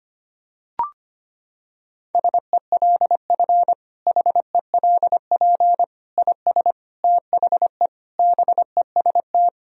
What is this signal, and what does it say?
Morse code, heard as a keyed tone, about 25 wpm, 700 Hz. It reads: SELF HELP IS THE BEST
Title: Sentences from Top 1000 Words - Review Part 1 - 25wpm